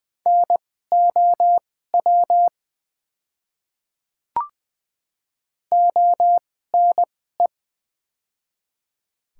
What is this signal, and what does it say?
Morse code, heard as a keyed tone, about 20 wpm, spaced as for 15 wpm, 700 Hz. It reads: NOW ONE
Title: 2x Character Spacing - Top 100 Words - 20wpm